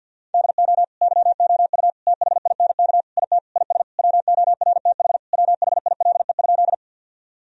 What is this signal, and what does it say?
Morse code, heard as a keyed tone, about 47 wpm, 700 Hz. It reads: DO YOU THINK IT IS WORTH WHILE?
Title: Sentences from Top 1000 Words - Review Part 1 - 50wpm